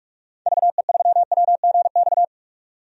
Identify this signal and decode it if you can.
Morse code, heard as a keyed tone, about 46 wpm, 700 Hz. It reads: VE3WGX
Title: DXCC-100 Call Signs - 6 Characters - 45wpm